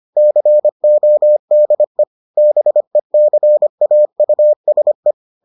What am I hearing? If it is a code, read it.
Morse code, heard as a keyed tone, about 25 wpm, 600 Hz. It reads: CODE BECAUSE